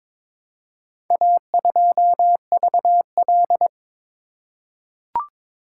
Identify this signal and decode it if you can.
Morse code, heard as a keyed tone, about 22 wpm, 700 Hz. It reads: A2VL
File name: practice / 22wpm / Four Letter-Number Combinations - 22wpm